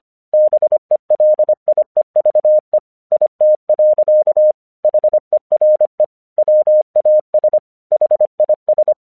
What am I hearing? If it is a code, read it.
Morse code, heard as a keyed tone, about 25 wpm, 610 Hz. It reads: BELIEVE IT. HERE WAS HIS
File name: 1 English 25wpm / AQC30